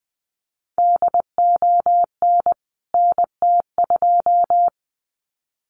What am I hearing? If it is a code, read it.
Morse code, heard as a keyed tone, about 20 wpm, 700 Hz. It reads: DON NT2
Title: CWT Exchange 20wpm